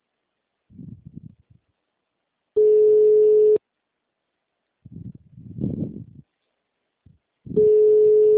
Ringback tone (Europe, 425 Hz).
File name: out- -2011- - - .742